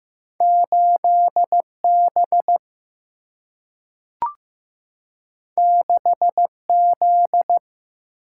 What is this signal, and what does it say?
Morse code, heard as a keyed tone, about 15 wpm, 700 Hz. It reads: 8 B 6 Z
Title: Single-Letter Single-Number 15wpm